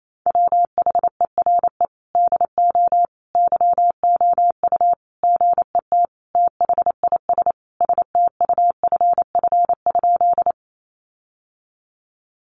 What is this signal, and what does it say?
Morse code, heard as a keyed tone, about 28 wpm, 700 Hz. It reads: WHERE DO YOU GET THIS STUFF?
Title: Sentences from Top 1400 Words 28wpm